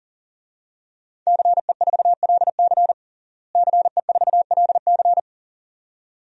Morse code, heard as a keyed tone, about 40 wpm, 700 Hz. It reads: CE4LC CE4LC
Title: DXCC-100 Call Signs - 5 Characters - Repeated 2x - 40wpm